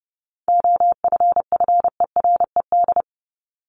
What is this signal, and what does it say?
Morse code, heard as a keyed tone, about 30 wpm, 700 Hz. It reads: OFFERED
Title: ED Suffix - Top 100 Words with Base Words 30wpm